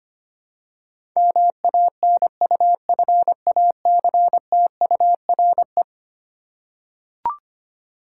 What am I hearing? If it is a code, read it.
Morse code, heard as a keyed tone, about 25 wpm, 700 Hz. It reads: MANUFACTURE
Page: morse-code-ninja-practice-25wpm